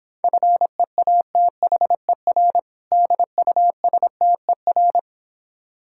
Morse code, heard as a keyed tone, about 26 wpm, 700 Hz. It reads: FEATHER DUSTER